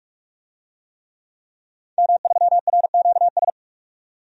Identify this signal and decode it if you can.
Morse code, heard as a keyed tone, about 42 wpm, 700 Hz. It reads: M3RQS
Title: DXCC-100 Call Signs - 5 Characters - 45wpm